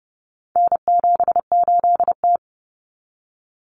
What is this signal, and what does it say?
Morse code, heard as a keyed tone, about 30 wpm, 700 Hz. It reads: N78T
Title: Common Cut Numbers - 4-Digit 30wpm